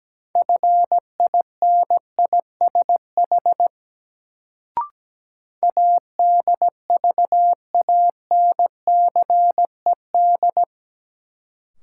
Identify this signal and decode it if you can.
Morse code, heard as a keyed tone, about 17 wpm, 700 Hz. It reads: FINISH ADVANCED
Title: Sentences from Top 1200 Words - New Word Drill Alt - 17wpm